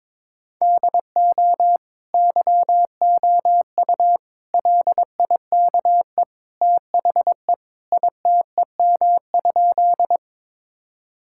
Morse code, heard as a keyed tone, about 22 wpm, 700 Hz. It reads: DO YOU LIKE THE ITEM?
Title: Sentences from Top 1000 Words - Review Part 6 - 22wpm